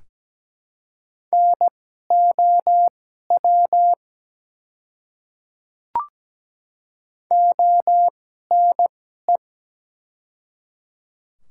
Morse code, heard as a keyed tone, about 17 wpm, spaced as for 12 wpm, 700 Hz. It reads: NOW ONE
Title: 2x Character Spacing - Top 100 Words - 17wpm